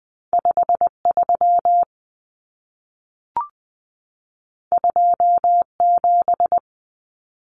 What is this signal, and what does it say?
Morse code, heard as a keyed tone, about 20 wpm, 700 Hz. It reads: 53 27
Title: Common Cut Numbers - 2-Digit 20wpm